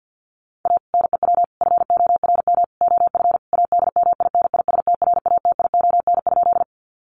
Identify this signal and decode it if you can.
Morse code, heard as a keyed tone, about 50 wpm, 700 Hz. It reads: A NEW FORM OF ADMINISTRATION?
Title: Sentences from Top 1300 Words 50wpm